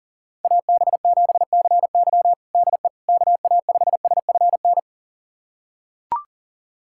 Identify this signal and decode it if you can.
Morse code, heard as a keyed tone, about 40 wpm, 700 Hz. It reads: AB7CY DE KA5SFN
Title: USA Callsign DE Callsign 40wpm